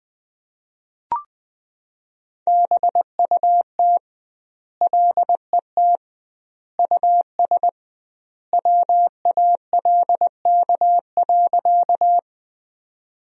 Morse code, heard as a keyed tone, about 20 wpm, 700 Hz. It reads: BUT LET US WALK.
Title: 2x Word-Spacing - Sentences from Top 300 Words - 20wpm